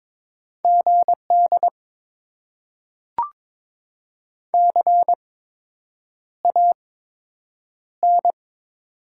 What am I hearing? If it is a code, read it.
Morse code, heard as a keyed tone, about 22 wpm, spaced as for 8 wpm, 700 Hz. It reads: GD CAN
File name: Top 90 QSO Elements - 8x Character Spacing - 22wpm